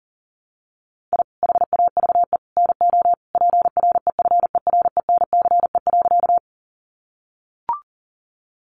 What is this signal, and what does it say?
Morse code, heard as a keyed tone, about 40 wpm, 700 Hz. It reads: I HAVE NO PREFERENCE.
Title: Sentences from Top 1500 Words 40wpm